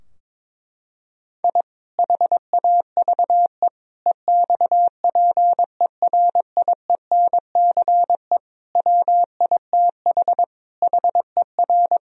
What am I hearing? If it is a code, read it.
Morse code, heard as a keyed tone, about 22 wpm, 700 Hz. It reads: I HAVE EXPERIENCE WITH HER
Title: Sentences from Top 1000 Words - Review Part 3 - 22wpm